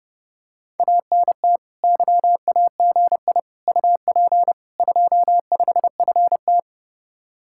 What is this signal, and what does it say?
Morse code, heard as a keyed tone, about 30 wpm, 700 Hz. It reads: ANT YAGI UP 25FT